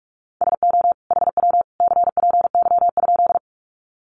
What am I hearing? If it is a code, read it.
Morse code, heard as a keyed tone, about 44 wpm, 700 Hz. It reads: SO HW CPY?
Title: Common QSO Phrases 45wpm